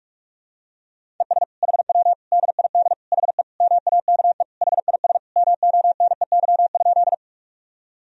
Morse code, heard as a keyed tone, about 46 wpm, 700 Hz. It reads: ES HW DID HE MAKE HIS MONEY?
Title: Sentences from Top 400 Words and Abbreviations 45wpm